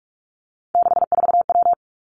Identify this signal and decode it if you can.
Morse code, heard as a keyed tone, about 46 wpm, 700 Hz. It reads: 64W
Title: Three Challenging Letter-Number Combinations 45wpm